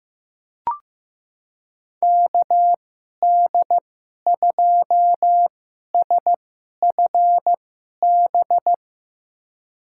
Morse code, heard as a keyed tone, about 15 wpm, spaced as for 11 wpm, 700 Hz. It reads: KD2SFB
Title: USA Call Signs - 6 Characters - 2x Character Spacing - 15wpm